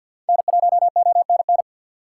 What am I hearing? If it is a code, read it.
Morse code, heard as a keyed tone, about 50 wpm, 700 Hz. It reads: N1ONN
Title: USA Call Signs - 5 Characters - 4x Character Spacing - 50wpm